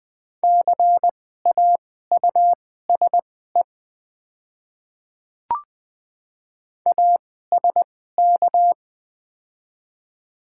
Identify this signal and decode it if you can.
Morse code, heard as a keyed tone, about 20 wpm, spaced as for 14 wpm, 700 Hz. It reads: CAUSE ASK
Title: 2x Character Spacing - Top 200 Words - 20wpm